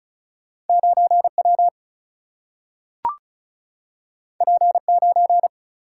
Morse code, heard as a keyed tone, about 36 wpm, 700 Hz. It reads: 9 W P 9
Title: Single-Letter Single-Number 35wpm